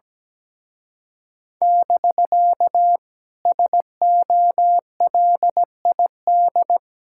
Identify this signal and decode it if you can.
Morse code, heard as a keyed tone, about 17 wpm, 700 Hz. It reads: <BK> SOLID
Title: Speed-Racing Common QSO Phrases - 17wpm